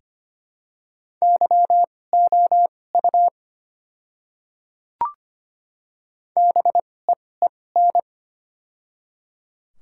Morse code, heard as a keyed tone, about 25 wpm, spaced as for 18 wpm, 690 Hz. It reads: YOU BEEN
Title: 2x Character Spacing - Top 100 Words - 25wpm